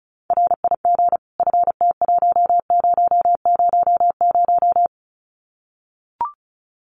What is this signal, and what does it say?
Morse code, heard as a keyed tone, about 34 wpm, 700 Hz. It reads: RIG FT1000
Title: Common QSO Phrases 35wpm